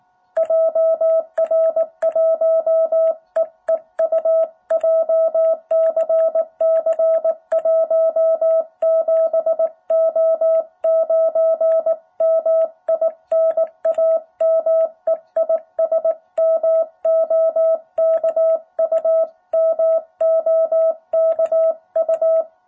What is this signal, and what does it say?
Morse code, heard as a keyed tone, about 18 wpm, 660 Hz. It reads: JR1EEU JCC 17O9 MI NAME IS MOKU MOKU
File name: jh8cbh / 2021cw